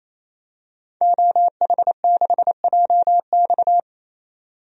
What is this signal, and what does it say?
Morse code, heard as a keyed tone, about 28 wpm, 700 Hz. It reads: OH6JX
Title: DXCC-100 Call Signs - 5 Characters - 28wpm